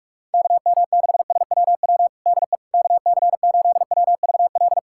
Morse code, heard as a keyed tone, about 46 wpm, 690 Hz. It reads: KM6SWW DE KC8WVL
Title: USA Callsign DE Callsign 45wpm